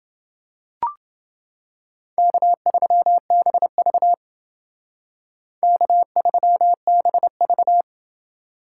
Morse code, heard as a keyed tone, about 29 wpm, 700 Hz. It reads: K3BV K3BV